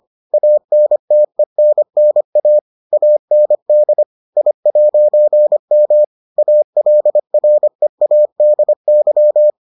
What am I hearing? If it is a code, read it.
Morse code, heard as a keyed tone, about 25 wpm, 600 Hz. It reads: ANTENNA AND I'M ALREADY